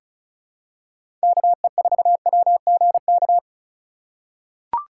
Morse code, heard as a keyed tone, about 36 wpm, 700 Hz. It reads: KE4WGK
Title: USA Call Signs - 6 Characters - 2x Character Spacing - 35wpm